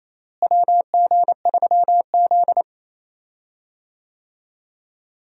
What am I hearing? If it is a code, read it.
Morse code, heard as a keyed tone, about 28 wpm, 700 Hz. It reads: WG3Z
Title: USA Call Signs - 4 Characters - Repeated 2x - 28wpm